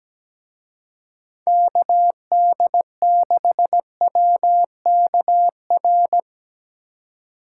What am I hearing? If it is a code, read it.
Morse code, heard as a keyed tone, about 18 wpm, 700 Hz. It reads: KD6WKR
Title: Speed Racing - USA 2x3 Call Signs 17wpm